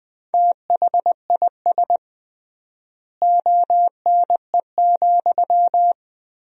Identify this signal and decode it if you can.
Morse code, heard as a keyed tone, about 21 wpm, 700 Hz. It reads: THIS ONE,
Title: 3x Word-Spacing - Sentences from Top 200 Words - 20wpm